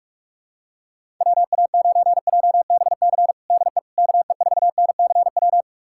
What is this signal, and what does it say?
Morse code, heard as a keyed tone, about 46 wpm, 700 Hz. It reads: WA9JBC DE KE4NCW
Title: USA Callsign DE Callsign - 6 Characters - 45wpm